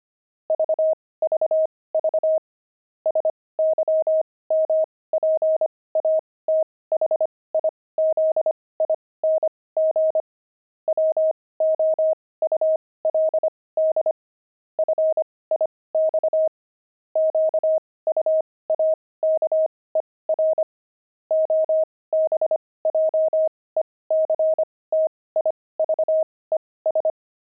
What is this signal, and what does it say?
Morse code, heard as a keyed tone, about 25 wpm, spaced as for 18 wpm, 630 Hz. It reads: VVV SYMPATHIZING WOULD FIX QUAKER OBJECTIVES
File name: pangram-fs-18-cs-25-quaker